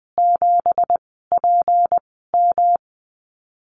Morse code, heard as a keyed tone, about 20 wpm, 700 Hz. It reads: 7 P M
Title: 179 Lesson 29g - Slash - Call Signs 20wpm